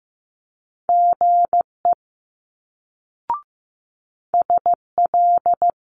Morse code, heard as a keyed tone, about 15 wpm, 700 Hz. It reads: GE SL
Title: Multi-letter Phonemes Quick Fire 15wpm